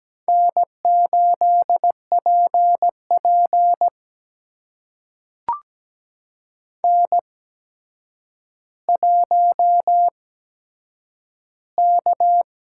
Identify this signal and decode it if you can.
Morse code, heard as a keyed tone, about 17 wpm, spaced as for 8 wpm, 700 Hz. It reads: N8PP N1K
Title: USA Call Signs - 4 Characters - 8x Character Spacing - 17wpm